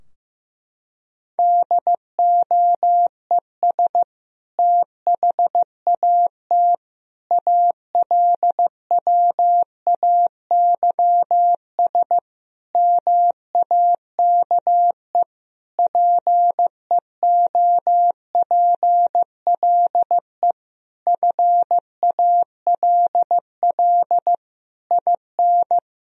Morse code, heard as a keyed tone, about 15 wpm, 700 Hz. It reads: DOES THAT ALWAYS MAKE PEOPLE FALL IN
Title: Sentences from Top 1000 Words - Review Part 2 - 15wpm